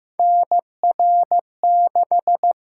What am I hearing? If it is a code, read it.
Morse code, heard as a keyed tone, about 15 wpm, 700 Hz. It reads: NR6
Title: USA Call Signs - 8x Character Spacing - 15wpm